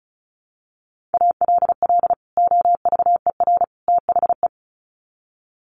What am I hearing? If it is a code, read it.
Morse code, heard as a keyed tone, about 34 wpm, 700 Hz. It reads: ALL OVER THE
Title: Sets of 3 Words 35wpm